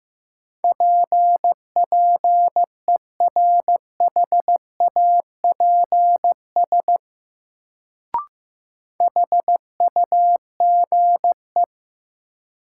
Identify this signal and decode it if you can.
Morse code, heard as a keyed tone, about 15 wpm, 700 Hz. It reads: PPERHAPS HUGE
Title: Sentences from Top 1100 Words - New Word Drill Alt - 15wpm